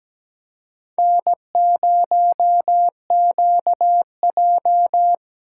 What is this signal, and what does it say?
Morse code, heard as a keyed tone, about 17 wpm, 700 Hz. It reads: N0QJ
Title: USA Call Signs - 4 Characters - 6x Character Spacing - 17wpm